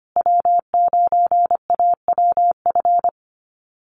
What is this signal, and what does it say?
Morse code, heard as a keyed tone, about 25 wpm, 700 Hz. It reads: W9AWF
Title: USA Call Signs Repeated 2x 25wpm